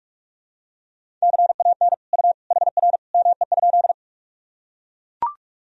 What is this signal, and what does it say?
Morse code, heard as a keyed tone, about 42 wpm, 700 Hz. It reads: CAN U HR ME?
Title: Sentences from Top 300 Words and Abbreviations 45wpm